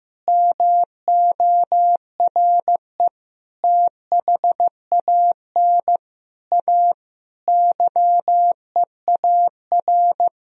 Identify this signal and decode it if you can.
Morse code, heard as a keyed tone, about 15 wpm, 700 Hz. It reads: MORE THAN A YEAR
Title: Sentences from Top 1000 Words - Review Part 3 - 15wpm